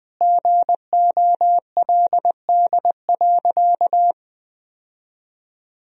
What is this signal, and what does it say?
Morse code, heard as a keyed tone, about 20 wpm, 700 Hz. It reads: GOLD.
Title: Sentences from Top 1000 Words - Review Part 3 - 20wpm